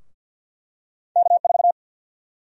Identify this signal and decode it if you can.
Morse code, heard as a keyed tone, about 49 wpm, 700 Hz. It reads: K4
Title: USA Call Sign Prefixes - Repeated 2x - 50wpm